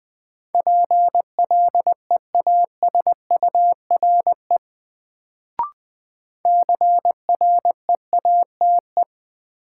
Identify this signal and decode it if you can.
Morse code, heard as a keyed tone, about 20 wpm, 700 Hz. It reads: PLEASURE CREATE
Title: Sentences from Top 1600 Words - New Word Drill Alt - 20wpm